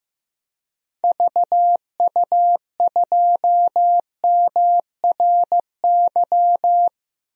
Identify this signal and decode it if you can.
Morse code, heard as a keyed tone, about 15 wpm, 700 Hz. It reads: VU2MRY
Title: DXCC-100 Call Signs - 6 Characters - 15wpm